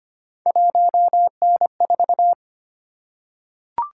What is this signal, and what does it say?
Morse code, heard as a keyed tone, about 25 wpm, 700 Hz. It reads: 1N4
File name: Three Letter-Number Combinations - 25wpm